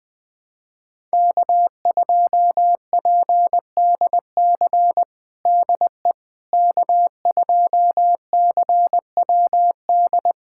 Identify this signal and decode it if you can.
Morse code, heard as a keyed tone, about 20 wpm, 700 Hz. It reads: K2PDC DE K2CWD
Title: USA Callsign DE Callsign - 5 Characters - 20wpm